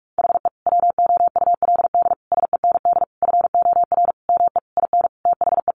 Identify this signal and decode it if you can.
Morse code, heard as a keyed tone, about 44 wpm, 700 Hz. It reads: HE WOULD SEND FOR ME IN THE
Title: Sentences from Top 1500 Words 45wpm